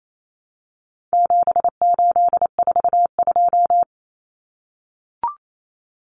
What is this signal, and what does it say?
Morse code, heard as a keyed tone, about 28 wpm, 700 Hz. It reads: 7842
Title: Numbers - 4-Digit 28wpm